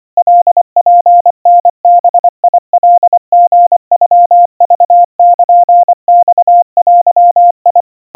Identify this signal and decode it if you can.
Morse code (Japanese Wabun code), heard as a keyed tone, about 24 wpm, 700 Hz. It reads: カツタバカリノクルマデ
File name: kawa / Morse sample / kamyu 90 cpm 700 Hz Crispy